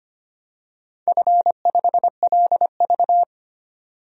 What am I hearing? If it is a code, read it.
Morse code, heard as a keyed tone, about 25 wpm, 700 Hz. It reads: F5LV
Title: DXCC-100 Call Signs - 4 Characters - 25wpm